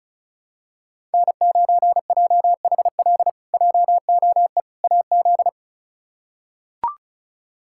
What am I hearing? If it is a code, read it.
Morse code, heard as a keyed tone, about 36 wpm, 700 Hz. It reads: N9JHL JOE AZ
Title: K1USN SST QSO Practice - 35wpm